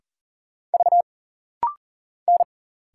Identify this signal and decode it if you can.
Morse code, heard as a keyed tone, about 40 wpm, 700 Hz. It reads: V N
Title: Single Letter-Number - Rapid-Fire 40wpm